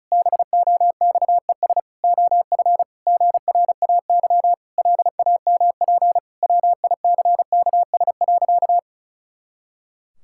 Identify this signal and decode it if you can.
Morse code, heard as a keyed tone, about 36 wpm, 700 Hz. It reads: BOXES OF GRAY LAMP WICKS.
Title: Pangram Challenge v2 - 35wpm